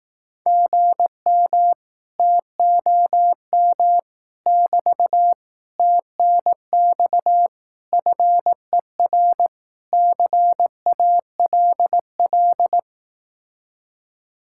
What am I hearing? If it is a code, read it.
Morse code, heard as a keyed tone, about 18 wpm, 700 Hz. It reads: GM TOM <BT> TNX FER CALL
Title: Speed-Racing Common QSO Phrases Encore - 15wpm